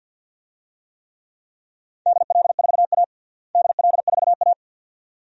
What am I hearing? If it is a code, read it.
Morse code, heard as a keyed tone, about 53 wpm, 700 Hz. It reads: DL4A DL4A
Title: DXCC-100 Call Signs - 4 Characters - Repeated 2x - 50wpm